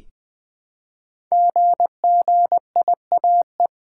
Morse code, heard as a keyed tone, about 20 wpm, 700 Hz. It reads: GGIAE
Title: Code Groups - 5 Letters - 20wpm